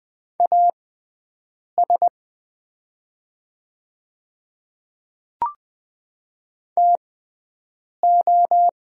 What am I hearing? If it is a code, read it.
Morse code, heard as a keyed tone, about 20 wpm, spaced as for 7 wpm, 700 Hz. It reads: AS TO
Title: 6x Character Spacing - Top 100 Words - 20wpm